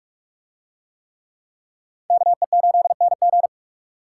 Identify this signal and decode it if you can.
Morse code, heard as a keyed tone, about 44 wpm, 700 Hz. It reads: KE8NG